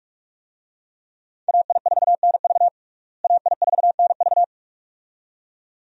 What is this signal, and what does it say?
Morse code, heard as a keyed tone, about 46 wpm, 700 Hz. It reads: AI4NV AI4NV
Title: USA Call Signs - 5 Characters - Repeated 2x - 45wpm